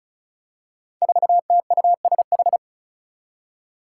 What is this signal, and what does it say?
Morse code, heard as a keyed tone, about 36 wpm, 700 Hz. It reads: 4TUSH
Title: Five Challenging Letter-Number Combinations 35wpm